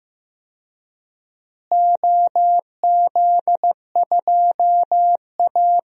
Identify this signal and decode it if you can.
Morse code, heard as a keyed tone, about 15 wpm, 700 Hz. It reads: OZ2A
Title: DXCC-100 Call Signs - 4 Characters - Repeated 3x - 15wpm